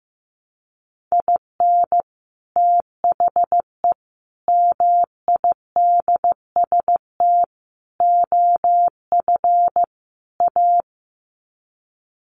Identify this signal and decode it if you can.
Morse code, heard as a keyed tone, about 15 wpm, 700 Hz. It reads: IN THE MIDST OF A
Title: Sets of 5 Words 15wpm